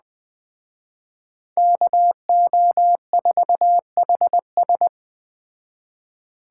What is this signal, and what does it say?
Morse code, heard as a keyed tone, about 21 wpm, 700 Hz. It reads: KO4HS